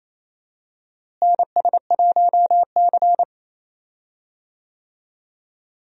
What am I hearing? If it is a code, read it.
Morse code, heard as a keyed tone, about 28 wpm, 700 Hz. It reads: NS1C
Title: USA Call Signs - 4 Characters - Repeated 3x - 28wpm